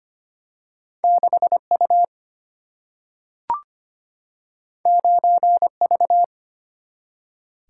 Morse code, heard as a keyed tone, about 25 wpm, 700 Hz. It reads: 6U 9V
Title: Single-Letter Single-Number 25wpm